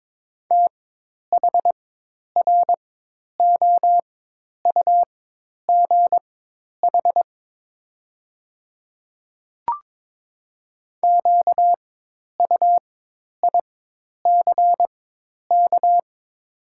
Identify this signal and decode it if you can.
Morse code, heard as a keyed tone, about 22 wpm, spaced as for 10 wpm, 700 Hz. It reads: THROUGH QUICK
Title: 4x Character Spacing - Top 500 Words - 22wpm